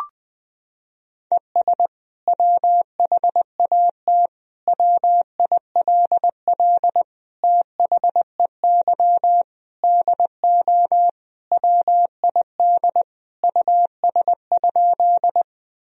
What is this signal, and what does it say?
Morse code, heard as a keyed tone, about 20 wpm, 700 Hz. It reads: ES WHAT WILL THEY DO WID US?
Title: Sentences from Top 200 Words and Abbreviations 20wpm